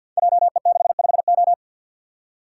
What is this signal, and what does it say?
Morse code, heard as a keyed tone, about 53 wpm, 700 Hz. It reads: JE65O
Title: Code Groups - 5 Letter:Numbers - 50wpm